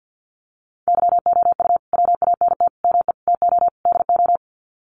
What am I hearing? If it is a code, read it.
Morse code, heard as a keyed tone, about 48 wpm, 700 Hz. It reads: YOU WANT ME TO DO
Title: Sets of 5 Words Encore2 50wpm